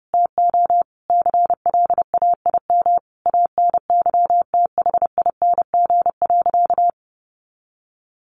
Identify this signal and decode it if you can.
Morse code, heard as a keyed tone, about 30 wpm, 700 Hz. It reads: TO CLAIM ANYTHING.
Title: Sentences from Top 1500 Words 30wpm